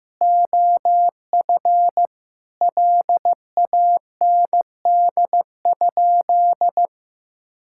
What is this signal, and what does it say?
Morse code, heard as a keyed tone, about 16 wpm, 700 Hz. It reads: OF LAND?
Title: Sentences from Top 1600 Words 15wpm